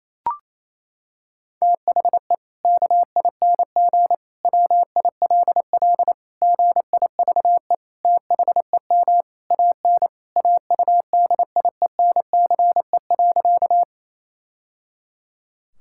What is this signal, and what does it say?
Morse code, heard as a keyed tone, about 27 wpm, 700 Hz. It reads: THE KING WILL GIVE THEM AN AUDIENCE.
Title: Sentences from Top 1000 Words - Review Part 5 - 28wpm